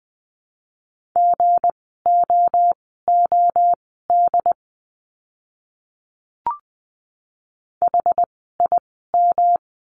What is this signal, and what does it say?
Morse code, heard as a keyed tone, about 20 wpm, spaced as for 14 wpm, 700 Hz. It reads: GOOD HIM
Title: 192 Lesson 31f - Letter v - Words 20wpm